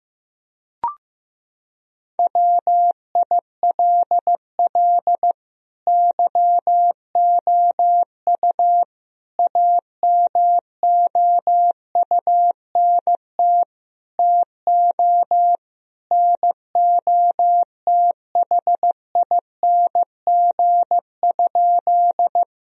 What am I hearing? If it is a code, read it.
Morse code, heard as a keyed tone, about 15 wpm, 700 Hz. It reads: WILL YOU AMOUNT TO NOTHING?
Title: Sentences from Top 1000 Words - Review Part 4 - 15wpm